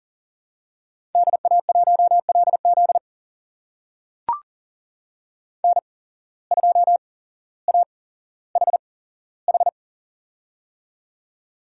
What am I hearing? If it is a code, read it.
Morse code, heard as a keyed tone, about 38 wpm, spaced as for 17 wpm, 700 Hz. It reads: DA1LZ N2AHH
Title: USA Call Signs - 5 Characters - 8x Character Spacing - 40wpm